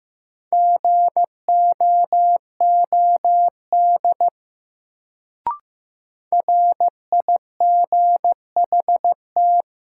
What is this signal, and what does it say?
Morse code, heard as a keyed tone, about 15 wpm, 700 Hz. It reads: GOOD RIGHT
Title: Sentences from Top 200 Words - New Word Drill - 15wpm